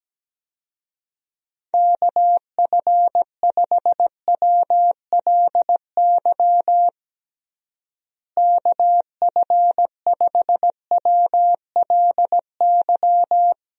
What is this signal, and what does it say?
Morse code, heard as a keyed tone, about 17 wpm, 700 Hz. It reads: KF5WLY KF5WLY
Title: USA Call Signs - 6 Characters - Repeated 3x - 17wpm